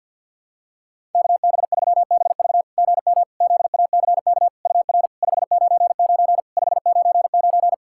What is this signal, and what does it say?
Morse code, heard as a keyed tone, about 48 wpm, 690 Hz. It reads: KB3BV GM ZACK UR 599 599
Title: SOTA QSO Practice 50wpm